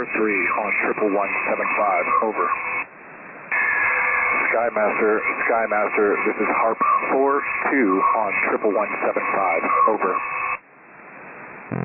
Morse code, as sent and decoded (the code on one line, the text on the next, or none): none